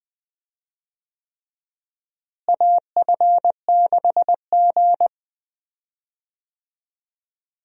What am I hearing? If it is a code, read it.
Morse code, heard as a keyed tone, about 21 wpm, 700 Hz. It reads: AF6G
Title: USA Call Signs - 4 Characters - Repeated 3x - 20wpm